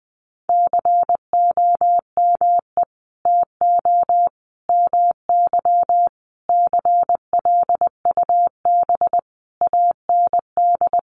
Morse code, heard as a keyed tone, about 20 wpm, 700 Hz. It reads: COME TO MY CLUB AND
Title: Sentences from Top 1500 Words 20wpm